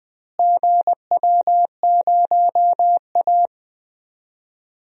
Morse code, heard as a keyed tone, about 20 wpm, 700 Hz. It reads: GW0A
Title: DXCC-100 Call Signs - 4 Characters - Repeated 2x - 20wpm